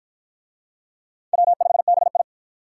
Morse code, heard as a keyed tone, about 52 wpm, 700 Hz. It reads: W5BI